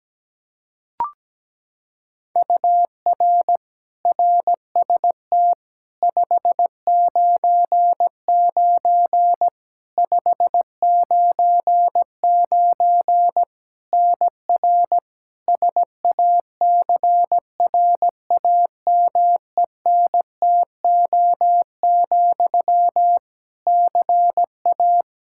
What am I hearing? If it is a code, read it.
Morse code, heard as a keyed tone, about 17 wpm, 700 Hz. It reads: UR RST 599 599 NR SACRAMENTO, CA
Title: K1USN SST QSO Practice 17wpm